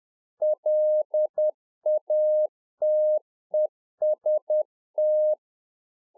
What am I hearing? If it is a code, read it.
Morse code, heard as a keyed tone, about 10 wpm, 610 Hz. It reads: LATEST
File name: 2011-04-mp3 / Czech Army Morse Keys 10WPM